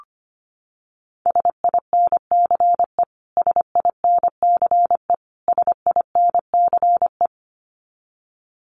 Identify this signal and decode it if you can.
Morse code, heard as a keyed tone, about 25 wpm, 700 Hz. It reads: SINCE SINCE SINCE
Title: Top 300 Words Repeated 25wpm